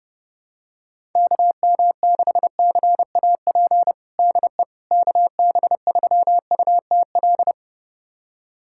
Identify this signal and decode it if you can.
Morse code, heard as a keyed tone, about 30 wpm, 700 Hz. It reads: KM6CAP DE KB3UTL
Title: USA Callsign DE Callsign 30wpm